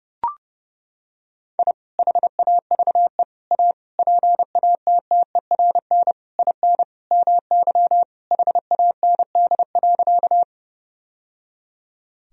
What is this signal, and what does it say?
Morse code, heard as a keyed tone, about 29 wpm, 700 Hz. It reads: I HAVE A PATTERN IN MY HAND.
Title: Sentences from Top 1000 Words - Review Part 2 - 30wpm